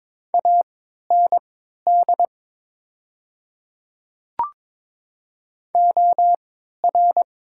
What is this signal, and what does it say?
Morse code, heard as a keyed tone, about 22 wpm, spaced as for 12 wpm, 700 Hz. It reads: AND OR
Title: 3x Character Spacing - Top 100 Words - 22wpm